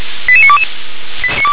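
Five-tone selective call (ZVEI), heard as quick runs of five tones, tones of about 70 ms. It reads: 88922
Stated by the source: ZVEI five-tone selective call 88953 received on SDR.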